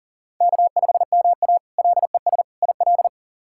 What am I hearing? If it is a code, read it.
Morse code, heard as a keyed tone, about 37 wpm, 700 Hz. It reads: K5MA LES IL
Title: K1USN SST QSO Practice - 40wpm